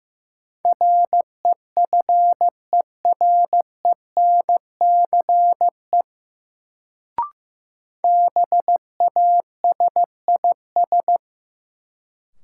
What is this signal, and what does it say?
Morse code, heard as a keyed tone, about 15 wpm, 700 Hz. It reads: REFERENCE BASIS
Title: Sentences from Top 1200 Words - New Word Drill Alt - 15wpm